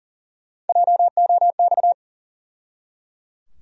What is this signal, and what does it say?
Morse code, heard as a keyed tone, about 40 wpm, 700 Hz. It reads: JOX
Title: USA Call Sign Postfixes - 3 Letters 40wpm